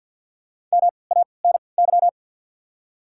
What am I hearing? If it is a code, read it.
Morse code, heard as a keyed tone, about 53 wpm, spaced as for 29 wpm, 700 Hz. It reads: MANY
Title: 3x Character Spacing - Top 200 Words - 50wpm